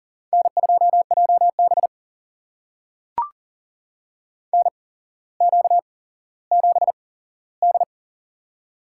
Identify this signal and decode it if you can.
Morse code, heard as a keyed tone, about 42 wpm, spaced as for 17 wpm, 700 Hz. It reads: N2JB NQ7D
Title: USA Call Signs - 4 Characters - 8x Character Spacing - 40wpm